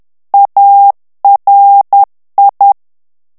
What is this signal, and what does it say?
Morse code, heard as a keyed tone, about 10 wpm, 800 Hz. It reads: ARI
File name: ITALIEN texte clair 2 vitesse 10 mots minute